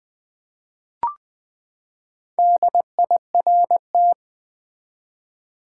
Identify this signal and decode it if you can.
Morse code, heard as a keyed tone, about 20 wpm, 700 Hz. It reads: DIRT
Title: Four Letter Words - 20wpm